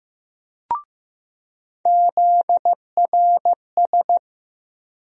Morse code, heard as a keyed tone, about 15 wpm, 700 Hz. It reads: ZRS
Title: USA Call Sign Postfixes - 3 Letters 15wpm